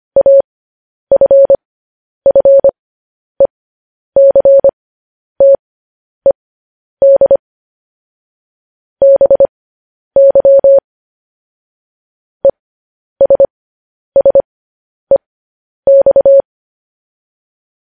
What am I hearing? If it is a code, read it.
Morse code, heard as a keyed tone, about 25 wpm, spaced as for 10 wpm, 550 Hz. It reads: AFFECTED BY ESSEX
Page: news-headlines-in-morse-code-at-10-wpm